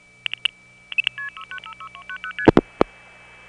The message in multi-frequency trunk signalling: 6690ST